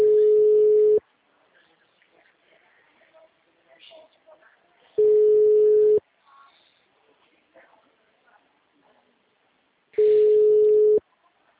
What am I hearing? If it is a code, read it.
Ringback tone (Europe, 425 Hz), with speech.